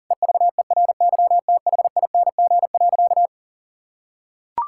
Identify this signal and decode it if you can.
Morse code, heard as a keyed tone, about 40 wpm, 700 Hz. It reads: EVERYTHING.
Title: Sentences from Top 1000 Words - Review Part 5 - 40wpm